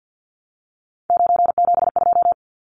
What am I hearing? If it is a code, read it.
Morse code, heard as a keyed tone, about 50 wpm, 700 Hz. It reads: 972
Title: Numbers - 3-Digit 50wpm